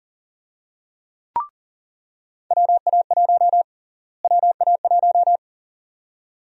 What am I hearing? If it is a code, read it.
Morse code, heard as a keyed tone, about 40 wpm, 700 Hz. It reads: WA1 WA1
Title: USA Call Sign Prefixes - Repeated 2x - 40wpm